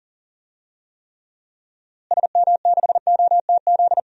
Morse code, heard as a keyed tone, about 40 wpm, 700 Hz. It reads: SM6OTZ
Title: DXCC-100 Call Signs - 6 Characters - Repeated 2x - 40wpm